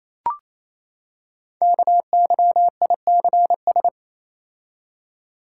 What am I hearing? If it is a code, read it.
Morse code, heard as a keyed tone, about 29 wpm, 700 Hz. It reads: KYICS
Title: Code Groups - 5 Letters - 28wpm